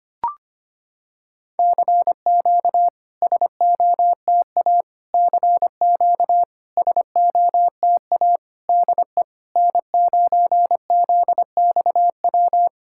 Morse code, heard as a keyed tone, about 25 wpm, 700 Hz. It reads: CQ SOTA CQ SOTA DE N9ZXW
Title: SOTA QSO Practice 25wpm